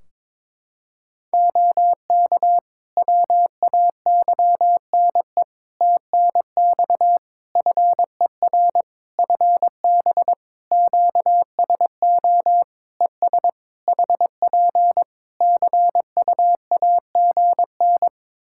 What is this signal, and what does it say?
Morse code, heard as a keyed tone, about 22 wpm, 700 Hz. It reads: OK WAYNE TNX FER FB QSO ES HP CUAGN
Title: Practice QSO - Exchange 4 - Ending - 22wpm